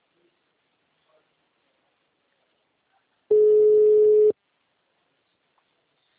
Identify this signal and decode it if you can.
Ringback tone (Europe, 425 Hz).